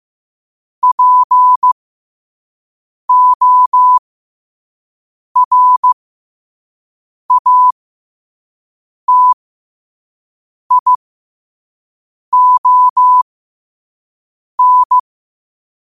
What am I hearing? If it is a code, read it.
Morse code, heard as a keyed tone, about 15 wpm, spaced as for 5 wpm, 1000 Hz. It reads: PORATION
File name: 5wpm